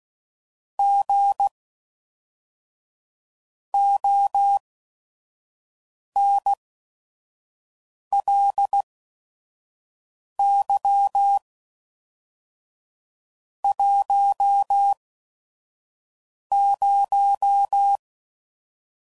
Morse code, heard as a keyed tone, about 16 wpm, spaced as for 5 wpm, 800 Hz. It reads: G ONLY 10
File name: CW Code Groups @5WPM / QSOs / kc7sac